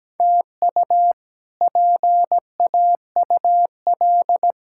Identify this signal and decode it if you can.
Morse code, heard as a keyed tone, about 17 wpm, 700 Hz. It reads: TU PAUL
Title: K1USN SST QSO Practice - 17wpm